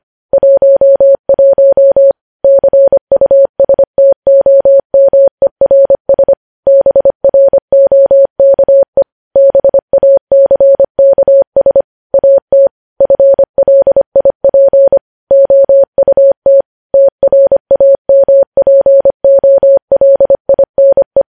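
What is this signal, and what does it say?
Morse code, heard as a keyed tone, about 25 wpm, 550 Hz. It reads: 11 CUSTOMERS BROKE BACKS AT FLIP OUT TRAMPOLINE
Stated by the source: News headlines sent in Morse at 25 wpm.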